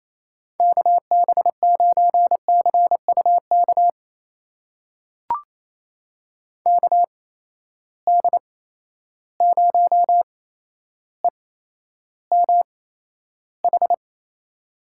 Morse code, heard as a keyed tone, about 28 wpm, spaced as for 12 wpm, 700 Hz. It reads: KB9CUK KD0EMH
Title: USA Call Signs - 6 Characters - 8x Character Spacing - 28wpm